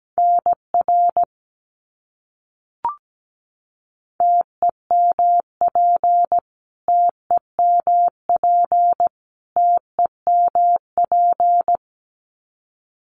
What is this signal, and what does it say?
Morse code, heard as a keyed tone, about 17 wpm, 700 Hz. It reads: NR TEMP TEMP TEMP
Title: Abbreviations Repeated 3x - 17wpm